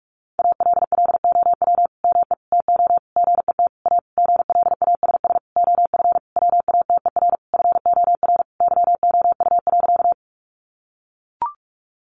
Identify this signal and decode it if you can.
Morse code, heard as a keyed tone, about 44 wpm, 700 Hz. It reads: ALLOW ME TO GET A GLASS OF WATER FOR YOU.
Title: Sentences from Top 1300 Words 45wpm